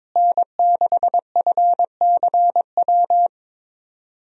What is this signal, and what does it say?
Morse code, heard as a keyed tone, about 22 wpm, 700 Hz. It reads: N6FCW
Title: USA Call Signs - 5 Characters - Repeated 3x - 22wpm